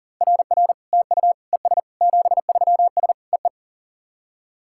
Morse code, heard as a keyed tone, about 40 wpm, 700 Hz. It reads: RR TU ES 73S EE
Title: SOTA QSO Practice 40wpm